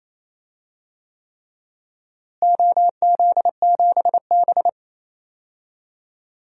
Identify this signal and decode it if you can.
Morse code, heard as a keyed tone, about 28 wpm, 700 Hz. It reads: OZ7B
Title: DXCC-100 Call Signs - 4 Characters - Repeated 3x - 28wpm